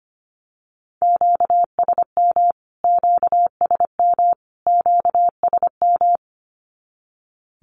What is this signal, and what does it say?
Morse code, heard as a keyed tone, about 25 wpm, 700 Hz. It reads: QSM QSM QSM
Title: Q-Codes Repeated 3x 25wpm